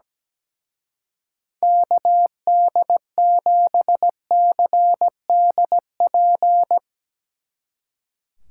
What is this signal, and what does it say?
Morse code, heard as a keyed tone, about 17 wpm, 700 Hz. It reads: KD7CDP
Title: Speed Racing - USA 2x3 Call Signs 17wpm